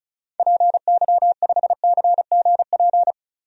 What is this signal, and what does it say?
Morse code, heard as a keyed tone, about 36 wpm, 700 Hz. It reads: PY5CGP
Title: DXCC-100 Call Signs - 6 Characters - Repeated 3x - 35wpm